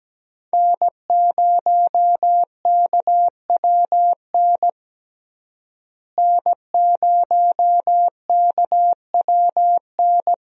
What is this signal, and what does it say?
Morse code, heard as a keyed tone, about 17 wpm, 700 Hz. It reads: N0KWN N0KWN
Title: USA Call Signs - 5 Characters - Repeated 2x - 17wpm